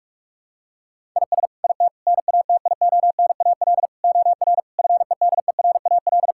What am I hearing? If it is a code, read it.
Morse code, heard as a keyed tone, about 46 wpm, 700 Hz. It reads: IS IT NATIONAL OR FEDERAL
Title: Sentences from Top 1000 Words - Review Part 5 - 45wpm